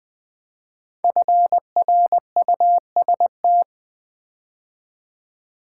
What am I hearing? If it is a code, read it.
Morse code, heard as a keyed tone, about 20 wpm, 700 Hz. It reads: FRUST
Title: Build A Word - 5 Letters At A Time - 10-Letter Words - 20wpm